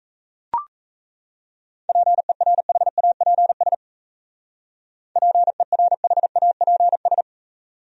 Morse code, heard as a keyed tone, about 33 wpm, 700 Hz. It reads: PERHAPS PERHAPS
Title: Speed-Racing Top 500 Words 28wpm